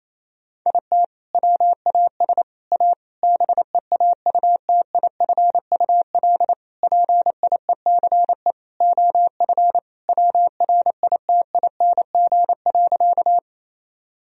Morse code, heard as a keyed tone, about 28 wpm, 700 Hz. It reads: IT WAS A BEAUTIFUL PIECE OF WRITING.
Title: Sentences from Top 1000 Words - Review Part 2 - 28wpm